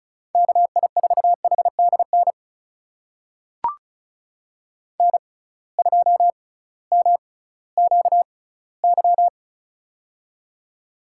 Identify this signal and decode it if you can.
Morse code, heard as a keyed tone, about 36 wpm, spaced as for 19 wpm, 700 Hz. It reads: KI4HDN N2MQY
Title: USA Call Signs - 6x Character Spacing - 35wpm